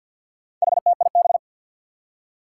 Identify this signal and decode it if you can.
Morse code, heard as a keyed tone, about 52 wpm, 700 Hz. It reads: HTIB
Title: Four Challenging Letter-Number Combinations 50wpm